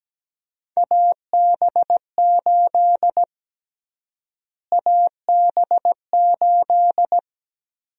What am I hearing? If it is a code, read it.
Morse code, heard as a keyed tone, about 17 wpm, 700 Hz. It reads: AB8 AB8
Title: USA Call Sign Prefixes - Repeated 2x - 17wpm